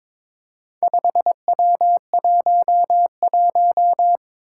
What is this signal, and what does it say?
Morse code, heard as a keyed tone, about 21 wpm, 700 Hz. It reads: 5W11
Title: Four Challenging Letter-Number Combinations 22wpm